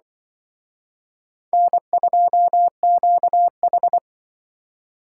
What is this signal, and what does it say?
Morse code, heard as a keyed tone, about 23 wpm, 700 Hz. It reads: N2QH
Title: Speed Racing - USA Call Signs 20wpm